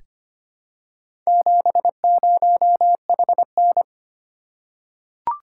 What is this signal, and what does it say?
Morse code, heard as a keyed tone, about 25 wpm, 700 Hz. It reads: 70HN
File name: practice / 25wpm / Four Letter-Number Combinations - 25wpm